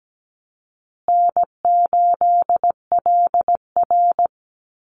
Morse code, heard as a keyed tone, about 17 wpm, 700 Hz. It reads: N8LR
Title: Easy Call Signs Repeated 3x - 17wpm